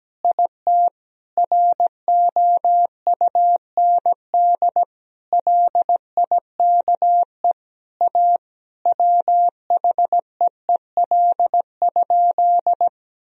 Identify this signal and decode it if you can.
Morse code, heard as a keyed tone, about 17 wpm, 700 Hz. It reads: IT ROUND LIKE A WHEEL?
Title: Sentences from Top 1000 Words - Review Part 3 - 17wpm